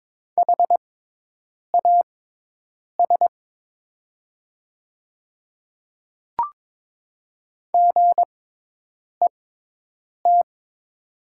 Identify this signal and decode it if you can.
Morse code, heard as a keyed tone, about 22 wpm, spaced as for 8 wpm, 700 Hz. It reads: HAS GET
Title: 6x Character Spacing - Top 200 Words - 22wpm